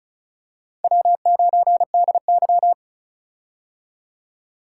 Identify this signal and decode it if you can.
Morse code, heard as a keyed tone, about 34 wpm, 700 Hz. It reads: W9DY
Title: K1USN SST QSO Practice - 35wpm